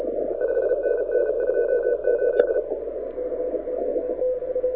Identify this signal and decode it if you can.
Morse code, heard as a keyed tone, about 36 wpm, 1500 Hz, with radio noise. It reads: 5NN 28
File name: audio zawody / IARU 2015 CW SN7Q